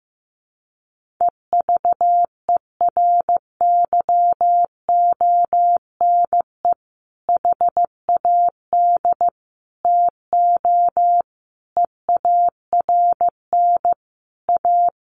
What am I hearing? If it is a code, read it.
Morse code, heard as a keyed tone, about 15 wpm, 700 Hz. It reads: EVERYONE HAD TO EARN A
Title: Sentences from Top 1500 Words 15wpm